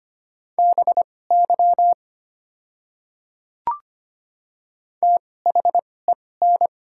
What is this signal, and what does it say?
Morse code, heard as a keyed tone, about 25 wpm, spaced as for 18 wpm, 700 Hz. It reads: BY THEN
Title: 2x Character Spacing - Top 100 Words - 25wpm